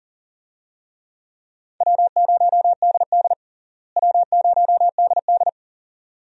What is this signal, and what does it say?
Morse code, heard as a keyed tone, about 40 wpm, 700 Hz. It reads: W0DD W0DD
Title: USA Call Signs - 4 Characters - Repeated 2x - 40wpm